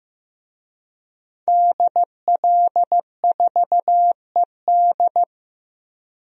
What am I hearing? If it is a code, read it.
Morse code, heard as a keyed tone, about 15 wpm, 700 Hz. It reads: DL4ED
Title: DXCC-100 Call Signs - 5 Characters - Repeated 2x - 15wpm